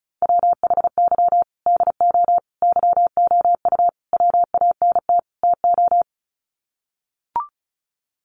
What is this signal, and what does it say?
Morse code, heard as a keyed tone, about 34 wpm, 700 Hz. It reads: WHY DO YOU WANT TO
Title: Sets of 5 Words Encore2 35wpm